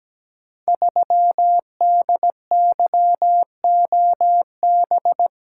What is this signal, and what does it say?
Morse code, heard as a keyed tone, about 17 wpm, 700 Hz. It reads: 3DYOB
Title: Code Groups - 5 Letter:Numbers - 17wpm